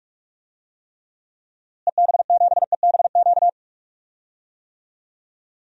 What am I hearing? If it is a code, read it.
Morse code, heard as a keyed tone, about 46 wpm, 700 Hz. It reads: EB7EBQ
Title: DXCC-100 Call Signs - 6 Characters - Repeated 3x - 45wpm